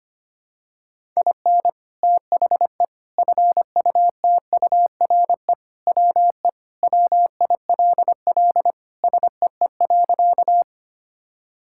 Morse code, heard as a keyed tone, about 25 wpm, 700 Hz. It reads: IN THE FUTURE WE WILL SEE.
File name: practice / 25wpm / Sentences with 3-Word Phrases - 25wpm